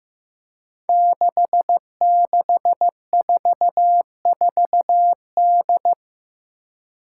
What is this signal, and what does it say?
Morse code, heard as a keyed tone, about 15 wpm, 700 Hz. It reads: 6644D
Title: Five Challenging Letter-Number Combinations 15wpm